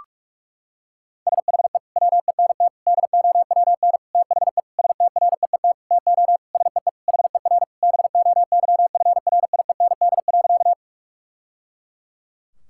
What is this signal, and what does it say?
Morse code, heard as a keyed tone, about 41 wpm, 690 Hz. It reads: SHE WENT DOWN THE STREET TO SEE HER BOYFRIEND.
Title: Sentences from Top 1000 Words - Review Part 5 - 45wpm